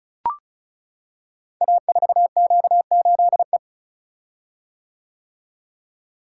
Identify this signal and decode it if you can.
Morse code, heard as a keyed tone, about 36 wpm, 690 Hz. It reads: A4Q8E
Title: Code Groups - 5 Letter:Numbers - 35wpm